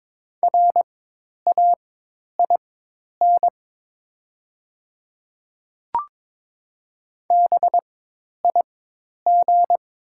Morse code, heard as a keyed tone, about 22 wpm, spaced as for 10 wpm, 690 Hz. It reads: RAIN BIG
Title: 4x Character Spacing - Top 500 Words - 22wpm